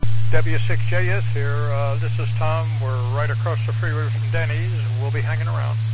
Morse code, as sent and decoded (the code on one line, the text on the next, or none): none